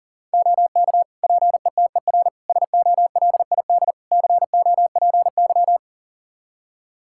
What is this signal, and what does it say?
Morse code, heard as a keyed tone, about 40 wpm, 700 Hz. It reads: OK PETER SOLID COPY
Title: K1USN SST QSO Practice 40wpm